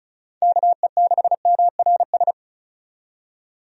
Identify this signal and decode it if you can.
Morse code, heard as a keyed tone, about 36 wpm, 700 Hz. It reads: KE6MRS
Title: USA Call Signs - 6 Characters - 35wpm